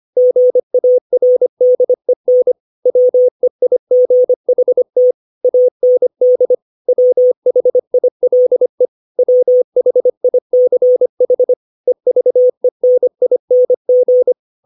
Morse code, heard as a keyed tone, about 25 wpm, 500 Hz. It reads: GARDEN WEIGHT AND WHILE WHICH EVENING
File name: 632 words / 632 Words 25WPM